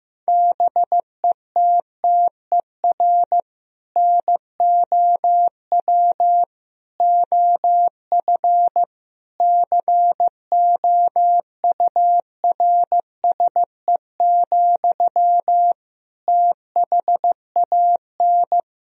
Morse code, heard as a keyed tone, about 15 wpm, 700 Hz. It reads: BETTER NOW OF COURSE, THAN
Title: Sentences from Top 1000 Words - Review Part 3 - 15wpm